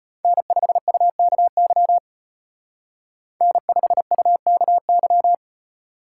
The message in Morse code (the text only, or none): N5UKY N5UKY